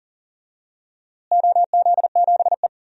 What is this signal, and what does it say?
Morse code, heard as a keyed tone, about 38 wpm, 700 Hz. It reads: OZ7E
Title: DXCC-100 Call Signs - 4 Characters - Repeated 3x - 40wpm